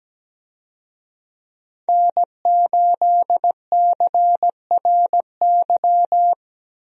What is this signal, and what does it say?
Morse code, heard as a keyed tone, about 17 wpm, 700 Hz. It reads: N8CRY
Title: USA Call Signs - 5 Characters - 17wpm